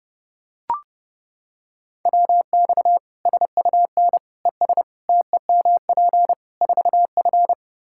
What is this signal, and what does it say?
Morse code, heard as a keyed tone, about 30 wpm, 700 Hz. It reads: WX SUN ES TEMP 4F
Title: Practice QSO - Exchange 2 - Info - 30wpm